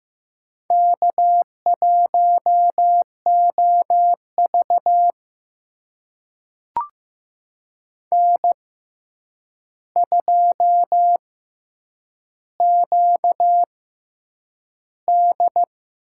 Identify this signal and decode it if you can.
Morse code, heard as a keyed tone, about 15 wpm, spaced as for 8 wpm, 700 Hz. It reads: K1OV N2QD
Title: USA Call Signs - 4 Characters - 6x Character Spacing - 15wpm